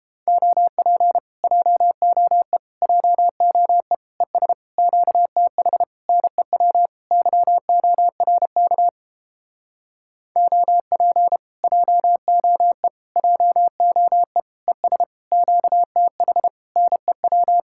Morse code, heard as a keyed tone, about 29 wpm, 700 Hz. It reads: OP JOE JOE ES QTH NEW YORK OP JOE JOE ES QTH NEW